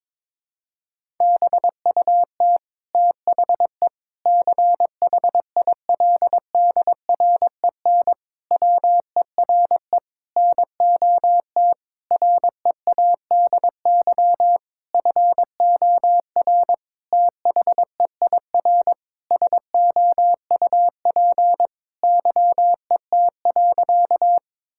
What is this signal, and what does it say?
Morse code, heard as a keyed tone, about 21 wpm, 700 Hz. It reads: BUT THE CHILDREN WERE NOT READY FOR THEIR SOUP YET.
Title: Sentences from Top 1000 Words - Review Part 6 - 22wpm